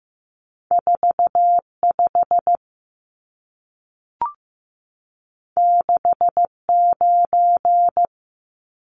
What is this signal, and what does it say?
Morse code, heard as a keyed tone, about 15 wpm, 700 Hz. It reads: 45 69
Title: Numbers - 2-Digit 15wpm